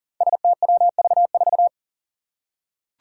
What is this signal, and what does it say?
Morse code, heard as a keyed tone, about 40 wpm, 700 Hz. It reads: STWV4
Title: Five Challenging Letter-Number Combinations 40wpm